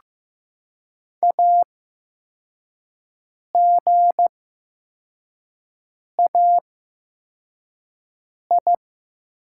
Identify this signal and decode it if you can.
Morse code, heard as a keyed tone, about 15 wpm, spaced as for 4 wpm, 700 Hz. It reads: AGAI
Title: 8x Character Spacing - Top 500 Words - 15wpm